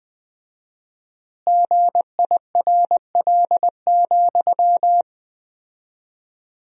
Morse code, heard as a keyed tone, about 20 wpm, 700 Hz. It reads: GIRL,
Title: 4x Word-Spacing - Sentences from Top 400 Words - 20wpm